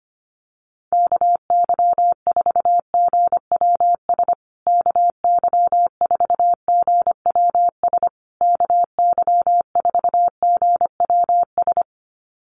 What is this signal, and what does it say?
Morse code, heard as a keyed tone, about 25 wpm, 700 Hz. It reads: KY4GWS KY4GWS KY4GWS
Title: USA Call Signs Repeated 3x 25wpm